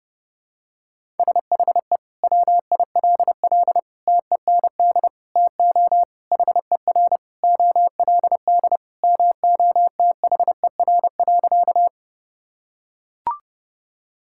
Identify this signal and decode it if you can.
Morse code, heard as a keyed tone, about 30 wpm, 700 Hz. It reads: SHE WILL TEND TO HER OLD MOTHER.
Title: Sentences from Top 1000 Words - Review Part 5 - 30wpm